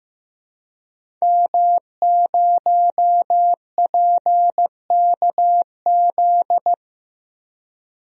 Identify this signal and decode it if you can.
Morse code, heard as a keyed tone, about 15 wpm, 700 Hz. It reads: M0PKZ
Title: DXCC-100 Call Signs - 5 Characters - 15wpm